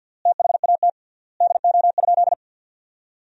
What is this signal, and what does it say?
Morse code, heard as a keyed tone, about 50 wpm, 700 Hz. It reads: THAT DO?
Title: 3x Word-Spacing - Sentences from Top 100 Words - 50wpm